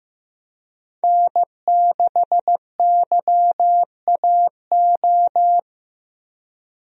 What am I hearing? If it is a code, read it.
Morse code, heard as a keyed tone, about 15 wpm, 700 Hz. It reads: N6YAO
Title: USA Call Signs - 5 Characters - 15wpm